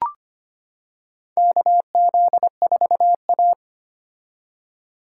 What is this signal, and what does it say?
Morse code, heard as a keyed tone, about 25 wpm, 700 Hz. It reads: KZ4A
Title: K1USN SST QSO Practice - 25wpm